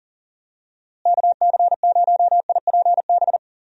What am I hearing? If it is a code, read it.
Morse code, heard as a keyed tone, about 38 wpm, 700 Hz. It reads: KC0IPB
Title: USA Call Signs - 6 Characters - Repeated 2x - 40wpm